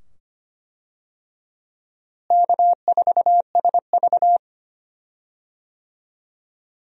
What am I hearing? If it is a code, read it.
Morse code, heard as a keyed tone, about 25 wpm, 700 Hz. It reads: K4SV
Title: USA Call Signs - 4 Characters - Repeated 2x - 25wpm